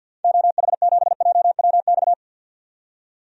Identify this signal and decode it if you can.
Morse code, heard as a keyed tone, about 52 wpm, 700 Hz. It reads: OH7JWX
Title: DXCC-100 Call Signs - 6 Characters - 50wpm